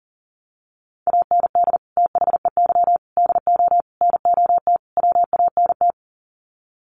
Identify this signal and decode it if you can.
Morse code, heard as a keyed tone, about 40 wpm, 700 Hz. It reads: AND THEY DO NOT WANT
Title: Sets of 5 Words Encore2 40wpm